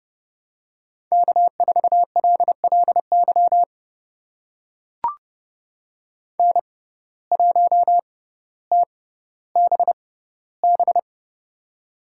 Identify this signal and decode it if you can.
Morse code, heard as a keyed tone, about 31 wpm, spaced as for 15 wpm, 700 Hz. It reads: K4LLY N1TBB
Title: USA Call Signs - 5 Characters - 6x Character Spacing - 30wpm